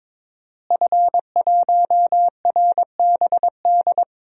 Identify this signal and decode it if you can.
Morse code, heard as a keyed tone, about 22 wpm, 700 Hz. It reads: F1RBD
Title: DXCC-100 Call Signs - 5 Characters - Repeated 2x - 22wpm